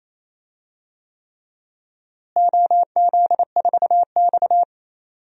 Morse code, heard as a keyed tone, about 27 wpm, 700 Hz. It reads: OZ4X
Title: DXCC-100 Call Signs - 4 Characters - Repeated 3x - 28wpm